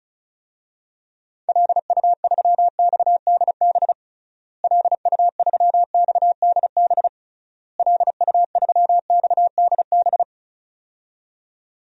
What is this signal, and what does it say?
Morse code, heard as a keyed tone, about 34 wpm, 700 Hz. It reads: LU3XDB LU3XDB LU3XDB
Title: DXCC-100 Call Signs - 6 Characters - Repeated 3x - 35wpm